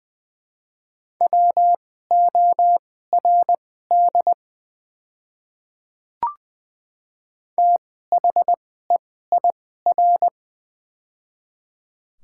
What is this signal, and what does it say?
Morse code, heard as a keyed tone, about 20 wpm, spaced as for 14 wpm, 700 Hz. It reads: WORD THEIR
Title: 2x Character Spacing - Top 100 Words - 20wpm